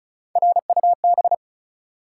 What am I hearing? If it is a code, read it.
Morse code, heard as a keyed tone, about 36 wpm, 700 Hz. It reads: RUB
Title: Three Letter Words - 35wpm